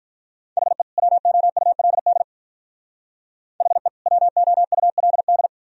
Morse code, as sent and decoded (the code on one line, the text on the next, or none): .... / .-- --- ..- .-.. -.. / .... / .-- --- ..- .-.. -..
HE WOULD HE WOULD